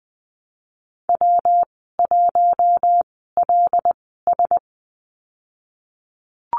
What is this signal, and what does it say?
Morse code, heard as a keyed tone, about 20 wpm, spaced as for 15 wpm, 700 Hz. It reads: W1LS
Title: 167 Lesson 27g - Number 7 - Call Signs 20wpm